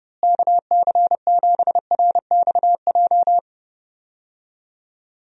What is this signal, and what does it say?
Morse code, heard as a keyed tone, about 32 wpm, 700 Hz. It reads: KC7RXJ
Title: USA Call Signs - 6 Characters - Repeated 3x - 30wpm